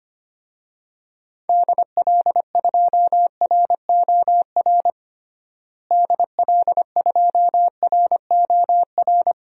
Morse code, heard as a keyed tone, about 25 wpm, 700 Hz. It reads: DL2ROR DL2ROR
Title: DXCC-100 Call Signs - 6 Characters - Repeated 2x - 25wpm